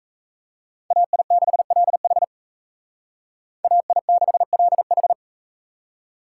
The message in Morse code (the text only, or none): AI6LH AI6LH